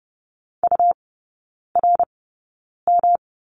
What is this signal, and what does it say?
Morse code, heard as a keyed tone, about 30 wpm, spaced as for 9 wpm, 700 Hz. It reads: URM
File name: Lesson 18e - Letter m - Three Characters 30wpm